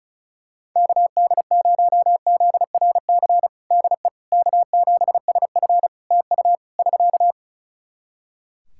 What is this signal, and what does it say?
Morse code, heard as a keyed tone, about 35 wpm, 700 Hz. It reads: KD0ZRC DE K7SF TU <SK>
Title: Practice QSO - Exchange 4 - Ending - 35wpm